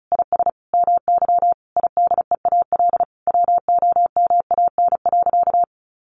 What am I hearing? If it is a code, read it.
Morse code, heard as a keyed tone, about 34 wpm, 700 Hz. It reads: IS MY IDEAL WOMAN.
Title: Sentences from Top 1400 Words 35wpm